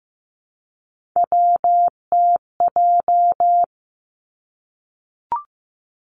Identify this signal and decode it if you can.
Morse code, heard as a keyed tone, about 15 wpm, 700 Hz. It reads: WTJ
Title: Three Challenging Letter-Number Combinations 15wpm